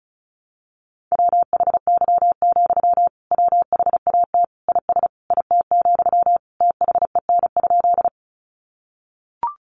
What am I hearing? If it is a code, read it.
Morse code, heard as a keyed tone, about 34 wpm, 700 Hz. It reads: WHY, WHAT IS IT, THEN?
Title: Sentences from Top 200 Words 35wpm